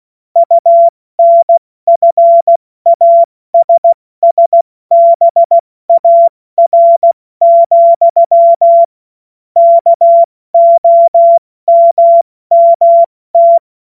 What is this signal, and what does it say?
Morse code, heard as a keyed tone, about 16 wpm, 680 Hz. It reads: UNFASSBAR, KOMMT
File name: Morsekurs / Witze / Witz 29. Juni 2021, 70 F80